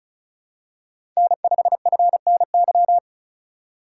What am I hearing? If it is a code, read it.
Morse code, heard as a keyed tone, about 36 wpm, 700 Hz. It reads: N5FNY
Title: USA Call Signs - 5 Characters - 3x Character Spacing - 35wpm